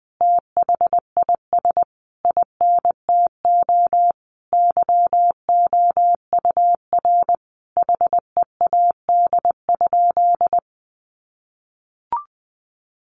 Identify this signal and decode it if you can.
Morse code, heard as a keyed tone, about 20 wpm, 700 Hz. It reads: THIS INTO YOUR HEAD?
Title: Sentences from Top 600 Words 20wpm